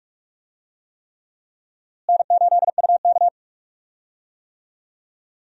Morse code, heard as a keyed tone, about 42 wpm, 700 Hz. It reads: N8UK
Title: USA Call Signs - 4 Characters - Repeated 3x - 45wpm